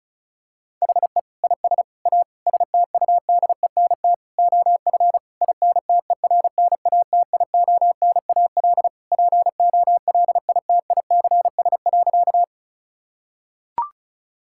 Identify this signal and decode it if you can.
Morse code, heard as a keyed tone, about 36 wpm, 700 Hz. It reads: HE IS A STUDENT OF INTERNATIONAL POLITICS.
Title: Sentences from Top 1000 Words - Review Part 6 - 35wpm